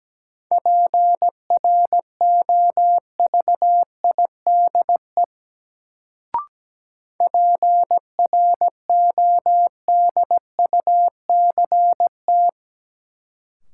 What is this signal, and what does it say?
Morse code, heard as a keyed tone, about 17 wpm, 700 Hz. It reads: PROVIDE PRODUCT
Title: Sentences from Top 1000 Words - Review Part 2 - New Word Drill - 17wpm